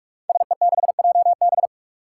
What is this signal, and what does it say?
Morse code, heard as a keyed tone, about 46 wpm, 700 Hz. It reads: SE6JB
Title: Five Challenging Letter-Number Combinations 45wpm